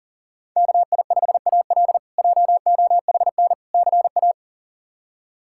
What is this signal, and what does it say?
Morse code, heard as a keyed tone, about 37 wpm, 700 Hz. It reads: KI5AL JOHN CA
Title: K1USN SST QSO Practice - 40wpm